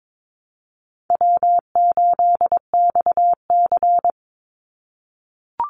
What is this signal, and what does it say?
Morse code, heard as a keyed tone, about 22 wpm, 700 Hz. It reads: W8XC
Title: Easy Call Signs - 22wpm